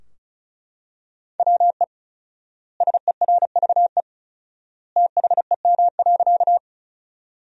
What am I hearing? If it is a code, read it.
Morse code, heard as a keyed tone, about 36 wpm, 700 Hz. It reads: WE SERVE THEM.
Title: 4x Word-Spacing - Sentences from Top 400 Words - 35wpm